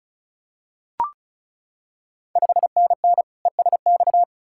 Morse code, heard as a keyed tone, about 36 wpm, 690 Hz. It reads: 5NN ESX
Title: Ontario QSO Party Exchange 35wpm